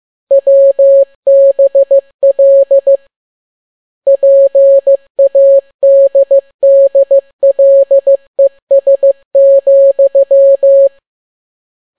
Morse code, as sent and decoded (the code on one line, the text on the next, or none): .-- -... .-.. / .--. .- -.. -.. .-.. ... --..--
WBL PADDLES,